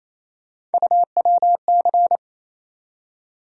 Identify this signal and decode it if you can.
Morse code, heard as a keyed tone, about 28 wpm, 700 Hz. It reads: UWC
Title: USA Call Sign Postfixes - 3 Letters 28wpm